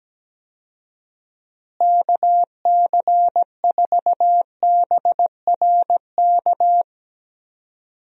Morse code, heard as a keyed tone, about 17 wpm, 700 Hz. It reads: KC4BRK
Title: USA Call Signs - 6 Characters - Repeated 2x - 17wpm